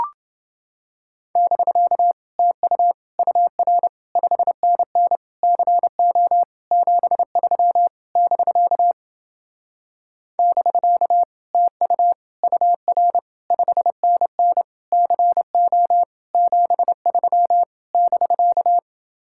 Morse code, heard as a keyed tone, about 29 wpm, 700 Hz. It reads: <BK> TU UR 5NN CO 73 <BK> <BK> TU UR 5NN CO 73 <BK>